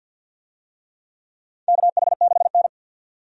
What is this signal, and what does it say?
Morse code, heard as a keyed tone, about 50 wpm, 700 Hz. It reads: KH6N
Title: DXCC-100 Call Signs - 4 Characters - 50wpm